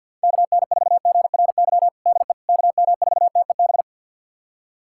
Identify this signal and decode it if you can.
Morse code, heard as a keyed tone, about 51 wpm, 700 Hz. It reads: KN4GRY DE KM4TEB
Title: USA Callsign DE Callsign 50wpm